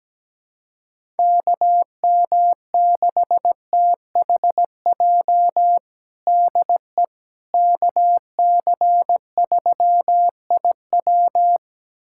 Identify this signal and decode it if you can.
Morse code, heard as a keyed tone, about 17 wpm, 700 Hz. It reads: KM6THJ DE KC3IW
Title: USA Callsign DE Callsign 17wpm